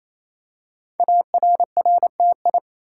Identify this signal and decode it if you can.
Morse code, heard as a keyed tone, about 29 wpm, 700 Hz. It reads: ARRTI
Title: Code Groups - 5 Letters - 28wpm